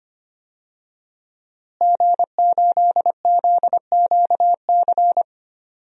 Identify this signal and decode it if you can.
Morse code, heard as a keyed tone, about 25 wpm, 700 Hz. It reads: G8ZQC
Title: DXCC-100 Call Signs - 5 Characters - Repeated 3x - 25wpm